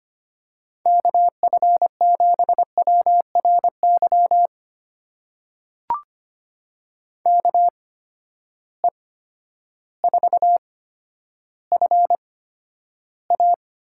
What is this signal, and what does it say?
Morse code, heard as a keyed tone, about 25 wpm, spaced as for 11 wpm, 700 Hz. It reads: KF7WRY KE4FA
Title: USA Call Signs - 6 Characters - 8x Character Spacing - 25wpm